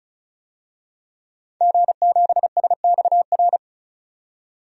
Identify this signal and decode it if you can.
Morse code, heard as a keyed tone, about 34 wpm, 700 Hz. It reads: G7SXR
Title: DXCC-100 Call Signs - 5 Characters - 35wpm